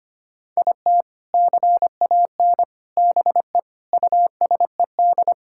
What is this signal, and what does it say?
Morse code, heard as a keyed tone, about 25 wpm, 700 Hz. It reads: IT CAN BE USED
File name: practice / 25wpm / Sentences with 4-Word Phrases - 25wpm